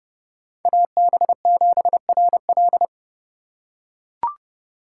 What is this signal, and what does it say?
Morse code, heard as a keyed tone, about 28 wpm, 700 Hz. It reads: AB7RL